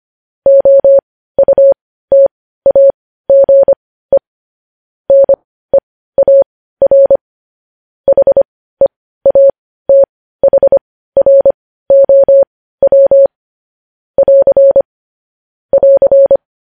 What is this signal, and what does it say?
Morse code, heard as a keyed tone, about 25 wpm, spaced as for 15 wpm, 550 Hz. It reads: OUTAGE NEAR HEATHROW <AR> <AR>
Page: news headlines in morse code at 15 wpm